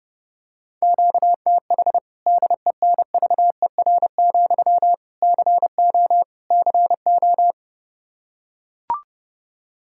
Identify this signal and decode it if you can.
Morse code, heard as a keyed tone, about 30 wpm, 700 Hz. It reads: QTH DENVER, CO CO
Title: Practice QSO - Exchange 1 - Beginning - 30wpm